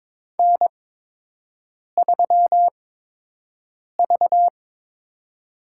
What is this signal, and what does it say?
Morse code, heard as a keyed tone, about 22 wpm, spaced as for 6 wpm, 700 Hz. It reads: N3V
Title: USA Call Signs - 4 Characters - 8x Character Spacing - 22wpm